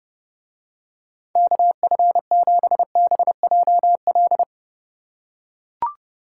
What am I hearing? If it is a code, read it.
Morse code, heard as a keyed tone, about 29 wpm, 700 Hz. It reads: KF7BJL